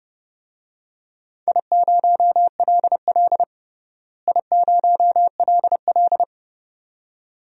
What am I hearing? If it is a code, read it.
Morse code, heard as a keyed tone, about 32 wpm, 700 Hz. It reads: I0LL I0LL
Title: DXCC-100 Call Signs - 4 Characters - Repeated 2x - 30wpm